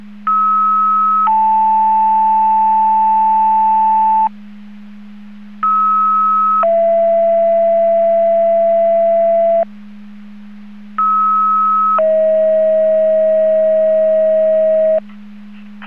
Two-tone page, heard as a one-second tone, then a three-second tone, about 1290 then 880 Hz, about 1290 then 670 Hz, about 1290 then 640 Hz. It is heard from 0.3 to 15.0 s.